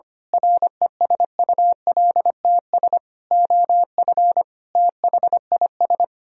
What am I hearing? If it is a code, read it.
Morse code, heard as a keyed tone, about 24 wpm, 700 Hz. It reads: RESULTS OF THIS
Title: Speed-Racing - Sets of 5 Words 25wpm